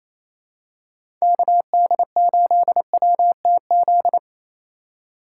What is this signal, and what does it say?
Morse code, heard as a keyed tone, about 28 wpm, 700 Hz. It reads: KD8WTZ
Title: USA Call Signs - 2x Character Spacing - 28wpm